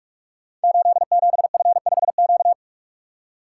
Morse code, heard as a keyed tone, about 44 wpm, 700 Hz. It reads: 87F5Q
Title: Code Groups - 5 Letter:Numbers - 45wpm